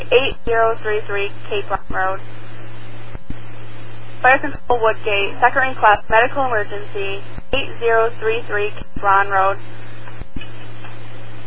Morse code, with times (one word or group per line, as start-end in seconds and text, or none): none